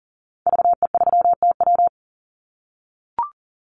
Morse code, heard as a keyed tone, about 40 wpm, 700 Hz. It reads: VE3TW
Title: Call Signs - 40wpm